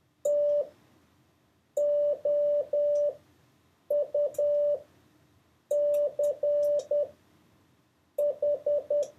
Morse code, heard as a keyed tone, about 10 wpm, spaced as for 6 wpm, 580 Hz. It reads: TOUCH
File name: 4of6-words-1